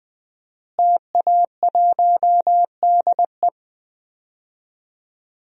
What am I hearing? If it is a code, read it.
Morse code, heard as a keyed tone, about 20 wpm, 700 Hz. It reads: TA1DE
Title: DXCC-100 Call Signs - 5 Characters - Repeated 2x - 20wpm